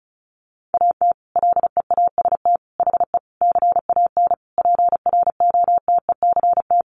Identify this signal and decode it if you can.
Morse code, heard as a keyed tone, about 34 wpm, 700 Hz. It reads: AT LEAST HE CAN PROTECT
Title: Sentences from Top 900 Words 35wpm